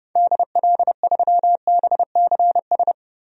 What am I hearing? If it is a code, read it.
Morse code, heard as a keyed tone, about 31 wpm, 700 Hz. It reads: DL3BCS
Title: DXCC-100 Call Signs - 6 Characters - 30wpm